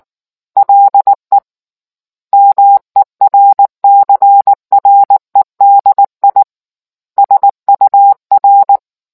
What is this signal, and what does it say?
Morse code, heard as a keyed tone, about 19 wpm, 800 Hz. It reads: LE MERCREDI SUR